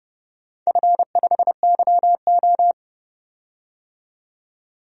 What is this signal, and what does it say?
Morse code, heard as a keyed tone, about 30 wpm, 700 Hz. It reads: F5YO
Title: DXCC-100 Call Signs - 4 Characters - 30wpm